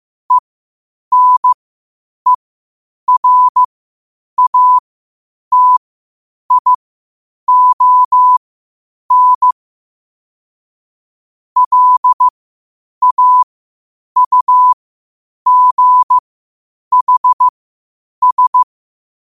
Morse code, heard as a keyed tone, about 15 wpm, spaced as for 8 wpm, 1000 Hz. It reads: ENERATION LAUGHS
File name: qotdfiles / 8wpm